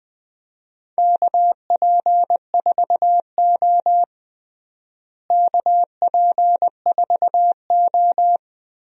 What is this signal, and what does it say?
Morse code, heard as a keyed tone, about 21 wpm, 700 Hz. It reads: KP4O KP4O
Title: DXCC-100 Call Signs - 4 Characters - Repeated 3x - 20wpm